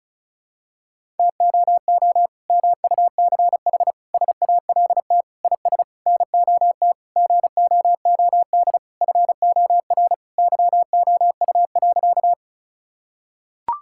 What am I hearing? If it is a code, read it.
Morse code, heard as a keyed tone, about 34 wpm, 700 Hz. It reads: TOO MUCH SALT IS NOT GOOD FOR YOU.
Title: Sentences from Top 1000 Words - Review Part 5 - 35wpm